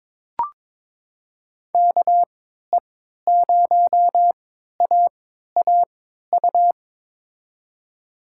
Morse code, heard as a keyed tone, about 22 wpm, spaced as for 13 wpm, 700 Hz. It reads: KE0AAU
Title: USA Call Signs - 6 Characters - 3x Character Spacing - 22wpm